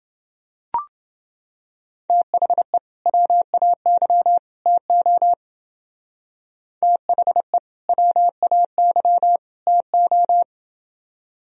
Morse code, heard as a keyed tone, about 29 wpm, 700 Hz. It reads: THE WAY TO THE WAY TO